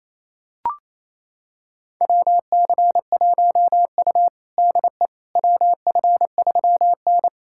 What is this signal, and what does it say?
Morse code, heard as a keyed tone, about 28 wpm, 700 Hz. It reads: WC1U DE WF3N
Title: USA Callsign DE Callsign - 4 Characters - 28wpm